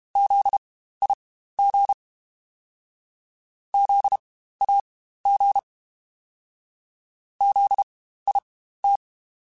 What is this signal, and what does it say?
Morse code, heard as a keyed tone, about 32 wpm, spaced as for 15 wpm, 800 Hz. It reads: ZIG ZAG ZIT
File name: Spaced Alphas / Spaced Alphas mp3s / Zs sp @30